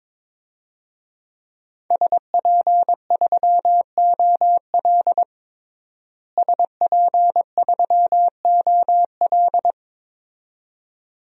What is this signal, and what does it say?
Morse code, heard as a keyed tone, about 22 wpm, 700 Hz. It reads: SP3OL SP3OL
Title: DXCC-100 Call Signs - 5 Characters - Repeated 2x - 22wpm